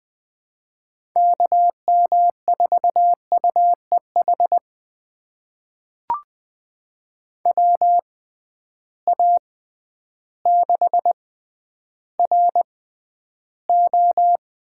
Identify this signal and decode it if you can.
Morse code, heard as a keyed tone, about 20 wpm, spaced as for 11 wpm, 700 Hz. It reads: KM4UEH WA6RO
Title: USA Call Signs - 6 Characters - 6x Character Spacing - 20wpm